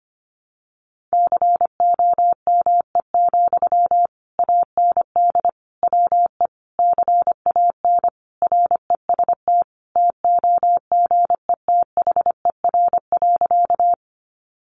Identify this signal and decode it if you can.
Morse code, heard as a keyed tone, about 25 wpm, 700 Hz. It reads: COME, AND WE CAN REST TOGETHER.
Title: Sentences from Top 500 Words 25wpm